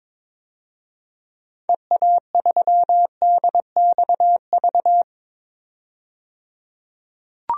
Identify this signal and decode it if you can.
Morse code, heard as a keyed tone, about 22 wpm, 700 Hz. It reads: EA3DXV
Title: DXCC-100 Call Signs - 6 Characters - 22wpm